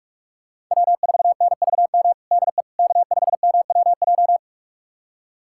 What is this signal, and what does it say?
Morse code, heard as a keyed tone, about 46 wpm, 700 Hz. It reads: W4NVM DE K5MWJ
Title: USA Callsign DE Callsign - 5 Characters - 45wpm